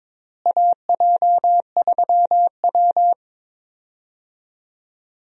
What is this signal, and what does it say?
Morse code, heard as a keyed tone, about 22 wpm, 700 Hz. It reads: AJ3W
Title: USA Call Signs - 4 Characters - Repeated 2x - 22wpm